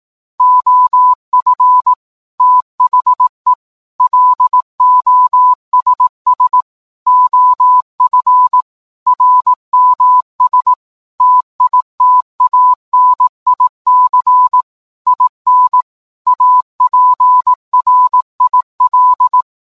Morse code, heard as a keyed tone, about 18 wpm, 1000 Hz. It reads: OF THE LOSS OF RMS TITANIC IN APRIL
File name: download-morsecode-titanic